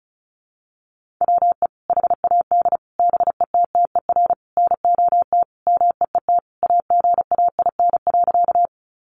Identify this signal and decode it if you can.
Morse code, heard as a keyed tone, about 34 wpm, 700 Hz. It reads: WE HAD BETTER NOT MEET AGAIN.
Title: Sentences from Top 1100 Words 35wpm